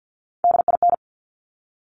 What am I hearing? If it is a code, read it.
Morse code, heard as a keyed tone, about 50 wpm, 700 Hz. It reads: DIN
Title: Top Three Letter Combinations in Words 50wpm